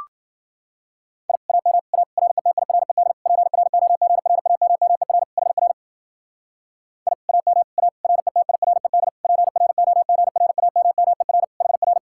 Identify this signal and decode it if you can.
Morse code, heard as a keyed tone, about 48 wpm, 700 Hz. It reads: I AM A RETIRED PROGRAMMER HR I AM A RETIRED PROGRAMMER HR
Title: Speed-Racing Common QSO Phrases Encore - 40wpm